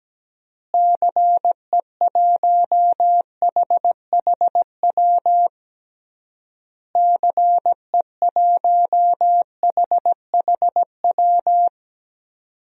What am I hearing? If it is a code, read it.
Morse code, heard as a keyed tone, about 17 wpm, 700 Hz. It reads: CE1HHW CE1HHW
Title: DXCC-100 Call Signs - 6 Characters - Repeated 3x - 17wpm